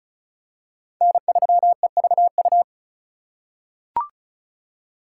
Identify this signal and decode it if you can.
Morse code, heard as a keyed tone, about 36 wpm, 700 Hz. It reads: N3EVU
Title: USA Call Signs - 5 Characters - 3x Character Spacing - 35wpm